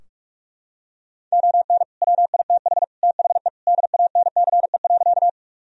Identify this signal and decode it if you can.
Morse code, heard as a keyed tone, about 46 wpm, 700 Hz. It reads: ON WITH THE DANCE.
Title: Sentences from Top 1000 Words - Review Part 4 - 45wpm